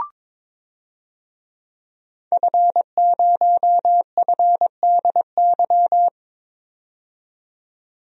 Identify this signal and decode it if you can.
Morse code, heard as a keyed tone, about 22 wpm, 700 Hz. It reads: F0FDY
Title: DXCC-100 Call Signs - 5 Characters - 22wpm